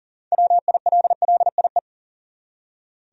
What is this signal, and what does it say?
Morse code, heard as a keyed tone, about 40 wpm, 700 Hz. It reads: WILLIE
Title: USA Common Names - 40wpm